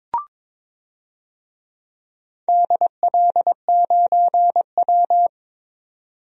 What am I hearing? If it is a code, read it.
Morse code, heard as a keyed tone, about 23 wpm, 700 Hz. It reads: DL9W
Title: DXCC-100 Call Signs - 4 Characters - 22wpm